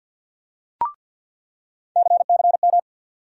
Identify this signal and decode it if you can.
Morse code, heard as a keyed tone, about 48 wpm, 700 Hz. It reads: CCM
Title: USA Call Sign Postfixes - 3 Letters 50wpm